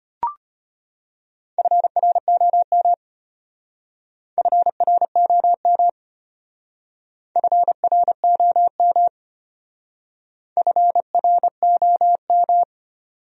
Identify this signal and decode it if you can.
Morse code, heard as a keyed tone, about 32 wpm, 700 Hz. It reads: FROM FROM FROM FROM